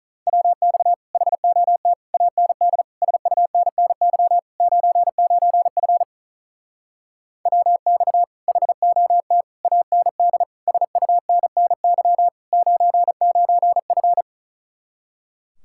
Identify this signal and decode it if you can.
Morse code, heard as a keyed tone, about 44 wpm, 700 Hz. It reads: WX HOT AND SUNNY 99F WX HOT AND SUNNY 99F